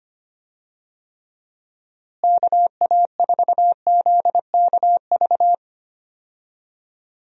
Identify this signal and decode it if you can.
Morse code, heard as a keyed tone, about 25 wpm, 700 Hz. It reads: KA4ZKV
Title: USA Call Signs - 6 Characters - Repeated 3x - 25wpm